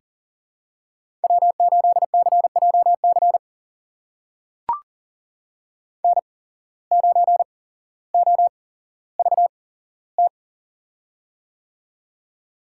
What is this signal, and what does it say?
Morse code, heard as a keyed tone, about 40 wpm, spaced as for 17 wpm, 700 Hz. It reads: W8CJC N9OVT
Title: USA Call Signs - 5 Characters - 8x Character Spacing - 40wpm